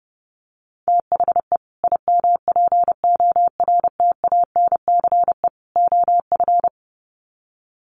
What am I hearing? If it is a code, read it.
Morse code, heard as a keyed tone, about 28 wpm, 700 Hz. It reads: THE IMPORTANCE OF
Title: Sets of 3 Words 30wpm